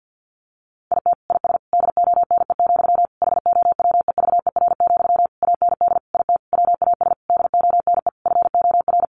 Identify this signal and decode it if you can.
Morse code, heard as a keyed tone, about 50 wpm, 700 Hz. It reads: IT IS DONE, HOWEVER, AND IT WAS DONE FOR
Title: Sentences from Top 600 Words 50wpm